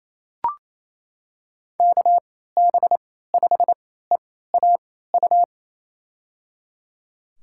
Morse code, heard as a keyed tone, about 28 wpm, spaced as for 16 wpm, 700 Hz. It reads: KB5EAU
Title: USA Call Signs - 6 Characters - 3x Character Spacing - 28wpm